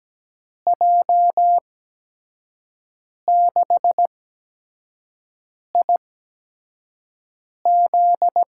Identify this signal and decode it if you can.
Morse code, heard as a keyed tone, about 17 wpm, spaced as for 5 wpm, 700 Hz. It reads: J6IZ
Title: USA Call Signs - 5 Characters - 8x Character Spacing - 17wpm